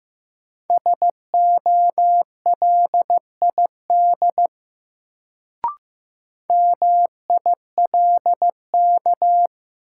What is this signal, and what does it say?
Morse code, heard as a keyed tone, about 15 wpm, 700 Hz. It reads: SOLID MILK
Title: Sentences from Top 1500 Words - New Word Drill - 15wpm